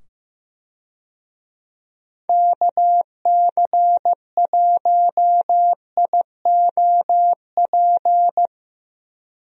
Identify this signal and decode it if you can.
Morse code, heard as a keyed tone, about 15 wpm, 700 Hz. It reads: KC1IOP
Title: USA Call Signs - 6 Characters - 15wpm